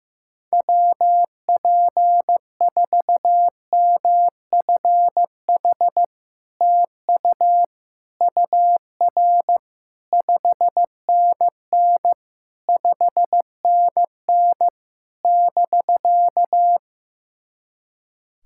Morse code, heard as a keyed tone, about 15 wpm, 690 Hz. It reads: WP4MFH TU UR 5NN 5NN <BK>
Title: SOTA QSO Practice 15wpm